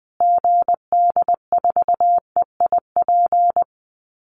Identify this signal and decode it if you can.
Morse code, heard as a keyed tone, about 20 wpm, 700 Hz. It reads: GD4EIP
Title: Call Signs Repeated 3x - 20wpm